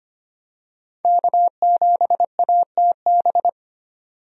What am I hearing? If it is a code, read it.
Morse code, heard as a keyed tone, about 25 wpm, 700 Hz. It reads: K7ATB
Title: USA Call Signs - 5 Characters - 25wpm